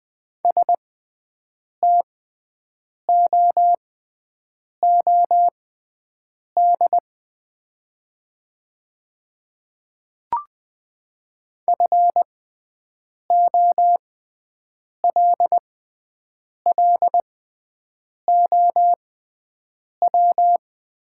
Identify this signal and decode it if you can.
Morse code, heard as a keyed tone, about 20 wpm, spaced as for 7 wpm, 700 Hz. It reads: STOOD FOLLOW
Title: 6x Character Spacing - Top 500 Words - 20wpm